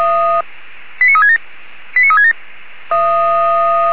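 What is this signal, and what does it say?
Five-tone selective call (ZVEI), heard as quick runs of five tones, tones of about 80 ms. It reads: 89367 89367
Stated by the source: ZVEI five-tone selective call 89366, received by SDR.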